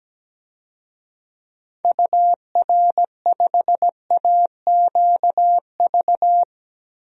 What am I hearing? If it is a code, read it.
Morse code, heard as a keyed tone, about 17 wpm, 700 Hz. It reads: UR5AQV
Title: DXCC-100 Call Signs - 6 Characters - Repeated 2x - 17wpm